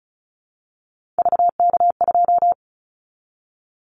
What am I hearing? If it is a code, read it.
Morse code, heard as a keyed tone, about 36 wpm, 700 Hz. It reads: VK2
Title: Callsign Prefixes for Top 50 DXCC - 35wpm